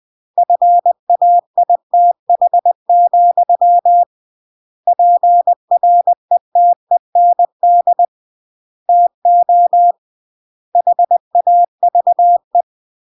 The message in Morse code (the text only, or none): FAITH, PRETEND TO HAVE